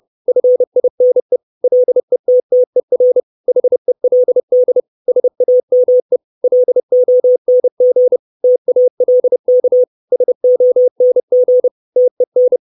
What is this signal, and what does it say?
Morse code, heard as a keyed tone, about 30 wpm, 500 Hz. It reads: FINE LETTER HELD SAME LONG TALK SONG TEN
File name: 632 Words 30WPM